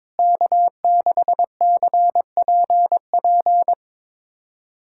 Morse code, heard as a keyed tone, about 22 wpm, 700 Hz. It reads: K6CPP
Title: USA Callsign DE Callsign - 5 Characters - 22wpm